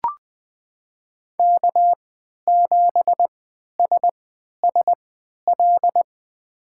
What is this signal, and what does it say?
Morse code, heard as a keyed tone, about 20 wpm, spaced as for 11 wpm, 700 Hz. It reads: K7SSL
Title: USA Call Signs - 5 Characters - 3x Character Spacing - 20wpm